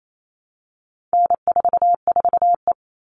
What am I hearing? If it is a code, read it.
Morse code, heard as a keyed tone, about 28 wpm, 700 Hz. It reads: N44E
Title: All Cut Numbers - 4-Digit 28wpm